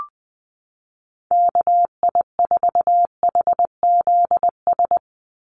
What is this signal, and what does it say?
Morse code, heard as a keyed tone, about 20 wpm, 700 Hz. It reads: KI4HZS
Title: USA Call Signs 20wpm